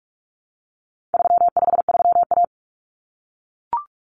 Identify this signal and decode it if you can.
Morse code, heard as a keyed tone, about 44 wpm, 700 Hz. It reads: 353A
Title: Common Cut Numbers - 4-Digit 45wpm